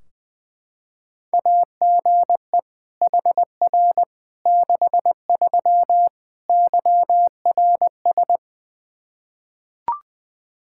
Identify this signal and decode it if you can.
Morse code, heard as a keyed tone, about 20 wpm, 700 Hz. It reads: AGE HR 63 YRS
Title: Practice QSO - Exchange 3 - More Info - 20wpm